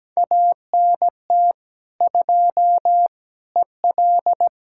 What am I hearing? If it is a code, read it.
Morse code, heard as a keyed tone, about 17 wpm, 700 Hz. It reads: ANT 2 EL
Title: Speed-Racing Common QSO Phrases - 17wpm